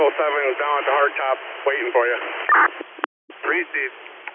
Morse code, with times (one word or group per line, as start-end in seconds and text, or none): none